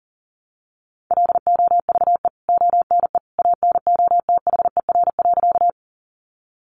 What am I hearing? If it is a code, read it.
Morse code, heard as a keyed tone, about 40 wpm, 700 Hz. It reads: LOVE ONE ANOTHER.
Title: Sentences from Top 600 Words 40wpm